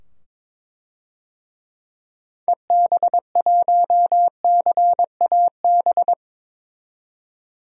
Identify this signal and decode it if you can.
Morse code, heard as a keyed tone, about 22 wpm, 700 Hz. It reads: EB1CAB
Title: DXCC-100 Call Signs - 6 Characters - 22wpm